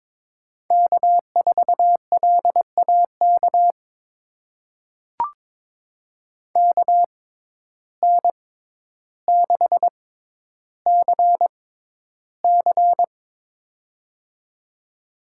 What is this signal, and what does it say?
Morse code, heard as a keyed tone, about 21 wpm, spaced as for 11 wpm, 700 Hz. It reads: K4LAK KN6CC
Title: USA Call Signs - 5 Characters - 6x Character Spacing - 22wpm